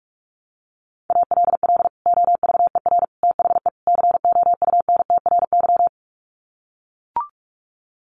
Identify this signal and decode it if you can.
Morse code, heard as a keyed tone, about 44 wpm, 700 Hz. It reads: ALL OVER THE COUNTRY
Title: Sets of 4 Words 45wpm